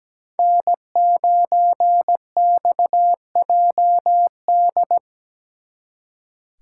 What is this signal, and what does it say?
Morse code, heard as a keyed tone, about 19 wpm, 700 Hz. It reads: N9XJD
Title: Speed Racing - USA 1x3 Call Signs 17wpm